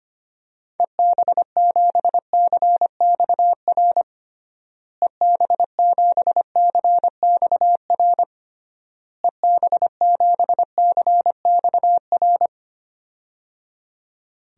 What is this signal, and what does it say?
Morse code, heard as a keyed tone, about 25 wpm, 700 Hz. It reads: EB7CXR EB7CXR EB7CXR
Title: DXCC-100 Call Signs - 6 Characters - Repeated 3x - 25wpm